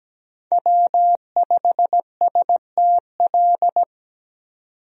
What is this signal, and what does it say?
Morse code, heard as a keyed tone, about 17 wpm, 700 Hz. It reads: W5STL
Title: USA Call Signs - 5 Characters - 17wpm